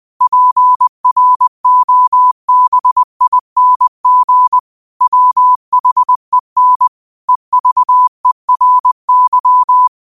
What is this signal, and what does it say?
Morse code, heard as a keyed tone, about 20 wpm, 1000 Hz. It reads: PROBING WHEN EVERY
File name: qotdfiles / 20wpm